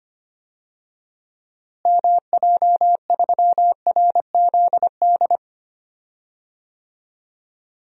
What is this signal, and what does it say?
Morse code, heard as a keyed tone, about 25 wpm, 700 Hz. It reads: MJ3RZD
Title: DXCC-100 Call Signs - 6 Characters - Repeated 3x - 25wpm